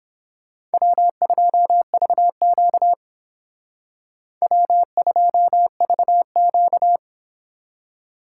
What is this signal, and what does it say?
Morse code, heard as a keyed tone, about 31 wpm, 700 Hz. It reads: W2VQ W2VQ